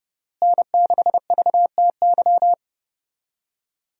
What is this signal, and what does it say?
Morse code, heard as a keyed tone, about 32 wpm, 700 Hz. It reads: N6VTY
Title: USA Call Signs - 5 Characters - Repeated 3x - 30wpm